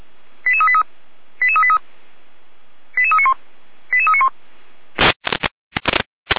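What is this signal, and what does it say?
Five-tone selective call (ZVEI), heard as quick runs of five tones, tones of about 70 ms. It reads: 88383 88383 88391 88391
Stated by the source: Received by SDR, ZVEI five-tone selective call, feuerwehr-orlamuende fire brigade (Orlamünde) audio archive.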